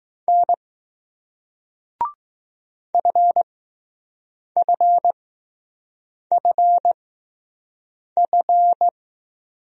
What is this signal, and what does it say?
Morse code, heard as a keyed tone, about 20 wpm, spaced as for 6 wpm, 700 Hz. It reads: N FFFF